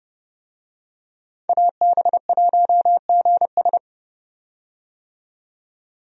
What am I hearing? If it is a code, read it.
Morse code, heard as a keyed tone, about 30 wpm, 700 Hz. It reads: AB1GS
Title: USA Call Signs - 5 Characters - 30wpm